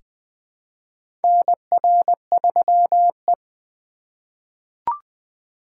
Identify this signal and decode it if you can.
Morse code, heard as a keyed tone, about 20 wpm, 700 Hz. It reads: NR3E
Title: 2023 SOTA Chaser Call Signs - 20wpm